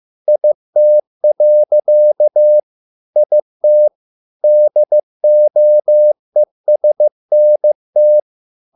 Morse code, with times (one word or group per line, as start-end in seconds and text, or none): 0.3-2.6 IT.
3.2-3.9 IT
4.4-8.2 DOESNT